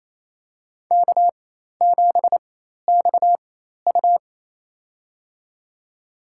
Morse code, heard as a keyed tone, about 28 wpm, spaced as for 13 wpm, 700 Hz. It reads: K7XU
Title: USA Call Signs - 4 Characters - 4x Character Spacing - 28wpm